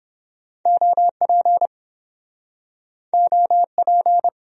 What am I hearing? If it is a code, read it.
Morse code, heard as a keyed tone, about 29 wpm, 700 Hz. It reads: OP OP